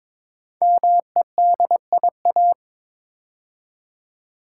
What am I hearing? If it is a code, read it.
Morse code, heard as a keyed tone, about 23 wpm, 700 Hz. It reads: MEDIA
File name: 5 Letter Words - First 3 Letters Given - 22wpm